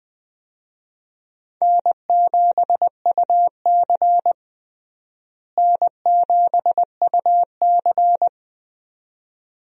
Morse code, heard as a keyed tone, about 20 wpm, 700 Hz. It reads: N7UC N7UC
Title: USA Call Signs - 4 Characters - Repeated 2x - 20wpm